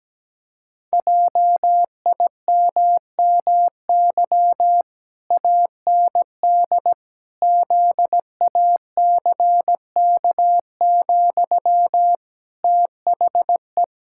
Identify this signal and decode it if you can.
Morse code, heard as a keyed tone, about 17 wpm, 700 Hz. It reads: JIMMY AND ZACK, THE
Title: Pangram Challenge v2 - 17wpm